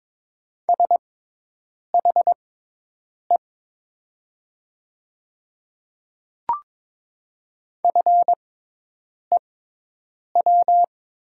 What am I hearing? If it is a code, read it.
Morse code, heard as a keyed tone, about 22 wpm, spaced as for 8 wpm, 700 Hz. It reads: SHE FEW
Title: 6x Character Spacing - Top 500 Words - 22wpm